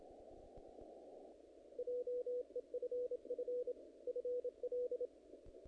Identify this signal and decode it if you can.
Morse code, heard as a keyed tone, about 27 wpm, 490 Hz. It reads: JEFF FL
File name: contests / naqp-cw-k5tr-op-k5pi- -1900